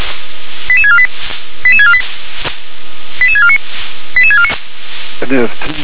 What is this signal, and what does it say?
Five-tone selective call (ZVEI), heard as quick runs of five tones, tones of about 70 ms, with speech. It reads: 88648 88648 88649 88649